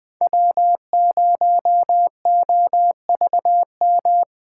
Morse code, heard as a keyed tone, about 21 wpm, 700 Hz. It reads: W0OVM
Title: Speed Racing - USA 1x3 Call Signs 17wpm